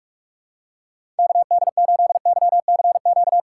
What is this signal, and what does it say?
Morse code, heard as a keyed tone, about 45 wpm, 700 Hz. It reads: KD8YCQ
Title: USA Call Signs - 6 Characters - 6x Character Spacing - 45wpm